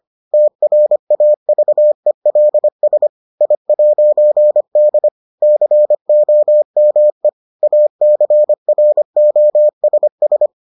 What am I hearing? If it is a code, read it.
Morse code, heard as a keyed tone, about 25 wpm, 600 Hz. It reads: TRAVELS I'D COME ACROSS